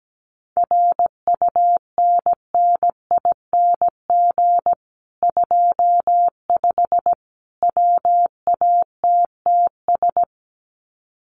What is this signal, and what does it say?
Morse code, heard as a keyed tone, about 17 wpm, 700 Hz. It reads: RUNNING 25 WATTS
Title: Common QSO Phrases 17wpm